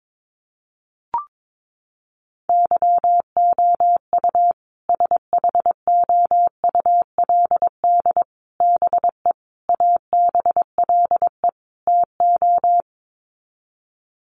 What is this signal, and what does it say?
Morse code, heard as a keyed tone, about 22 wpm, 700 Hz. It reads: YOU SHOULD BE ABLE TO
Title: Sets of 5 Words Encore 22wpm